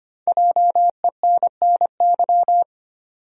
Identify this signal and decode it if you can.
Morse code, heard as a keyed tone, about 25 wpm, 700 Hz. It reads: JENNY
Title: USA Common Names - 25wpm